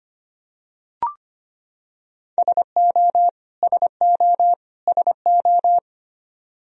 Morse code, heard as a keyed tone, about 25 wpm, 700 Hz. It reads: SO SO SO
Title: Top 100 Words Repeated 25wpm